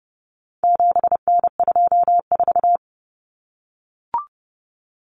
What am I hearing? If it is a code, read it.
Morse code, heard as a keyed tone, about 30 wpm, 700 Hz. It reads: 7N24
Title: Common Cut Numbers - 4-Digit 30wpm